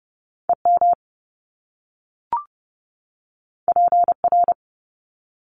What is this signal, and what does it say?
Morse code, heard as a keyed tone, about 30 wpm, 700 Hz. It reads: EM PR
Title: Top Two Letter Combinations in Words 30wpm